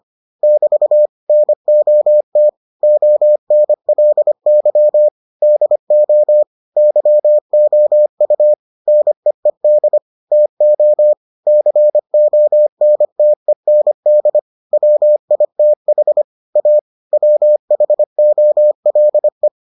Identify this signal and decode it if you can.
Morse code, heard as a keyed tone, about 25 wpm, 600 Hz. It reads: <BT> NOT ONLY DO YOU NEED TO CONTEND WITH A WHOLE